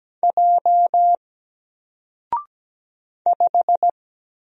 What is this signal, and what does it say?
Morse code, heard as a keyed tone, about 17 wpm, 700 Hz. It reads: J 5
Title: Single Letter-Number 17wpm